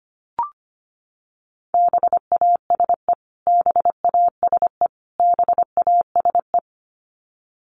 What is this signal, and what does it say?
Morse code, heard as a keyed tone, about 25 wpm, 700 Hz. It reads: BASE BASE BASE
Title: Top 500 Words Repeated 25wpm